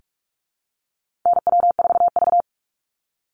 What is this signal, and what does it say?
Morse code, heard as a keyed tone, about 44 wpm, 700 Hz. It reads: NW4V
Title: Call Signs Repeated 3x - 45wpm